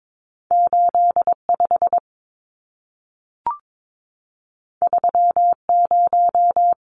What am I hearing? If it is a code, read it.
Morse code, heard as a keyed tone, about 22 wpm, 700 Hz. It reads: 85 30
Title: Numbers - 2-Digit 22wpm